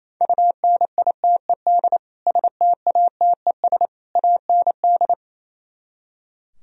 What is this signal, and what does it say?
Morse code, heard as a keyed tone, about 35 wpm, 700 Hz. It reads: UNITED STATES AND
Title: Speed-Racing - Sets of 3 Words 28wpm